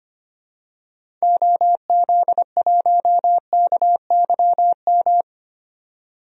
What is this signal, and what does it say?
Morse code, heard as a keyed tone, about 25 wpm, 700 Hz. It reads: OZ1KYM
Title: DXCC-100 Call Signs - 6 Characters - 25wpm